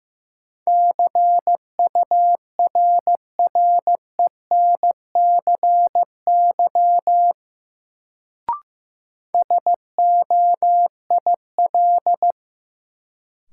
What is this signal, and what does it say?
Morse code, heard as a keyed tone, about 15 wpm, 700 Hz. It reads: CURRENCY SOIL
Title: Sentences from Top 1500 Words - New Word Drill Alt - 15wpm